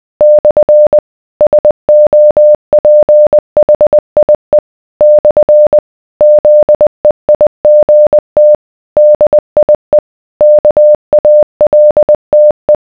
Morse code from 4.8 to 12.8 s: / ZEIGT DIE KALTE